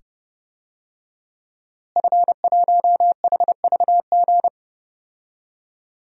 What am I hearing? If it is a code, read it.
Morse code, heard as a keyed tone, about 30 wpm, 700 Hz. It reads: F1HVG
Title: DXCC-100 Call Signs - 5 Characters - Repeated 2x - 30wpm